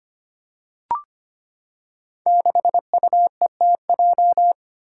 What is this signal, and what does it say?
Morse code, heard as a keyed tone, about 25 wpm, 700 Hz. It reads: 6UETJ
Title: Five Challenging Letter-Number Combinations 25wpm